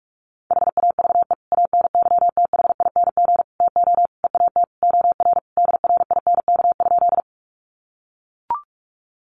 Morse code, heard as a keyed tone, about 44 wpm, 700 Hz. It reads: HAVE ANYTHING TO EAT OR DRINK?
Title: Sentences from Top 1500 Words 45wpm